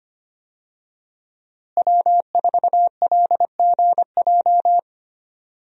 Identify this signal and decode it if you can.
Morse code, heard as a keyed tone, about 25 wpm, 700 Hz. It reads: W4LGJ
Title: USA Call Signs - 5 Characters - Repeated 2x - 25wpm